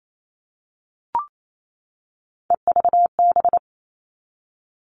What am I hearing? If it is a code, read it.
Morse code, heard as a keyed tone, about 28 wpm, 700 Hz. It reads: EVB
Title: All Cut Numbers - 3-Digit 28wpm